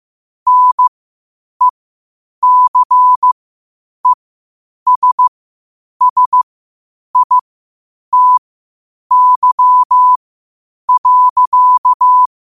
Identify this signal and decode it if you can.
Morse code, heard as a keyed tone, about 15 wpm, spaced as for 8 wpm, 1000 Hz. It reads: NECESSITY.
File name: qotdfiles / 8wpm